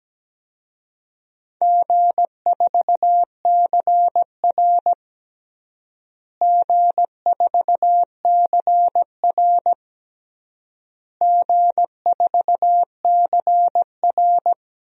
Morse code, heard as a keyed tone, about 17 wpm, 700 Hz. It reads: G4CR G4CR G4CR
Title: DXCC-100 Call Signs - 4 Characters - Repeated 3x - 17wpm